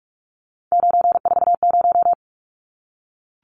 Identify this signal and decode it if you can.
Morse code, heard as a keyed tone, about 44 wpm, 700 Hz. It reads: 940
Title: Numbers - 3-Digit 45wpm